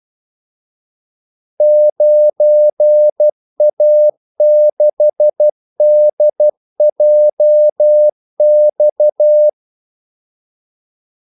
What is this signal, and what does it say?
Morse code, heard as a keyed tone, about 12 wpm, 600 Hz. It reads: 9A6DJX
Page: 2 ekstra ordmellemrum mellem kaldesignalerne